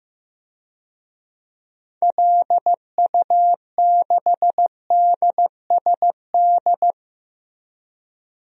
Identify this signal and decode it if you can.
Morse code, heard as a keyed tone, about 15 wpm, 700 Hz. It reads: LU6DSD
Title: DXCC-100 Call Signs - 6 Characters - Repeated 3x - 15wpm